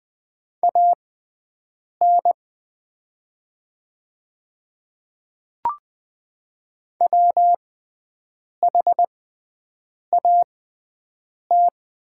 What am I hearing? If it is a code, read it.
Morse code, heard as a keyed tone, about 20 wpm, spaced as for 7 wpm, 690 Hz. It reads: AN WHAT
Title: 6x Character Spacing - Top 100 Words - 20wpm